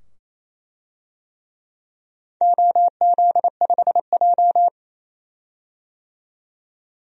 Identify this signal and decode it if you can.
Morse code, heard as a keyed tone, about 28 wpm, 700 Hz. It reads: OZ5J
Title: DXCC-100 Call Signs - 4 Characters - 28wpm